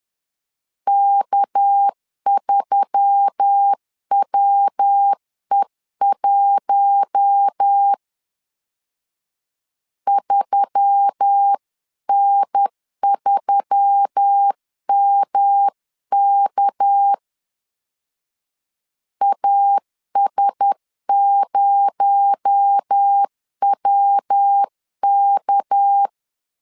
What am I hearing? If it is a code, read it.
Morse code, heard as a keyed tone, about 10 wpm, 800 Hz. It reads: K3WE1 3N3MK AS0WK
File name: CW / DET / 1 6 1500C S C 10wpm 4252